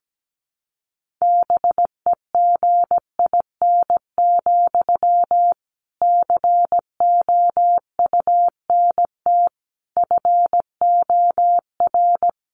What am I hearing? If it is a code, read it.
Morse code, heard as a keyed tone, about 17 wpm, 700 Hz. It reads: BEGIN, COUNT FOR
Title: Sentences from Top 1500 Words 17wpm